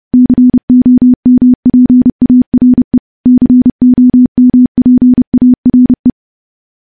Morse code, heard as a keyed tone, about 28 wpm, 250 Hz. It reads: COMPARE COMPARE
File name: morse / morse-training-30WPM